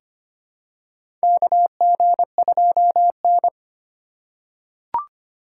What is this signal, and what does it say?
Morse code, heard as a keyed tone, about 25 wpm, 700 Hz. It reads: KG2N
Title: USA Call Signs - 4x Character Spacing - 25wpm